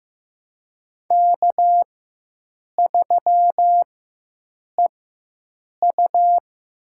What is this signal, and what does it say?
Morse code, heard as a keyed tone, about 15 wpm, spaced as for 7 wpm, 700 Hz. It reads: K3EU
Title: USA Call Signs - 4x Character Spacing - 15wpm